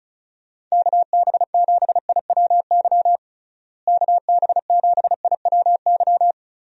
Morse code, heard as a keyed tone, about 34 wpm, 700 Hz. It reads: KB7IWY KB7IWY
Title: USA Call Signs - 6 Characters - Repeated 3x - 35wpm